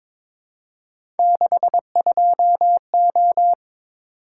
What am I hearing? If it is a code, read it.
Morse code, heard as a keyed tone, about 22 wpm, 700 Hz. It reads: 62O
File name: practice / 22wpm / Three Letter-Number Combinations - 22wpm